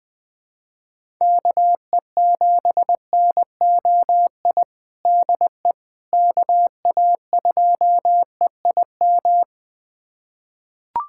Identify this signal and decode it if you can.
Morse code, heard as a keyed tone, about 21 wpm, 700 Hz. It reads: KE7NOI DE KA2EIM
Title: USA Callsign DE Callsign - 6 Characters - 20wpm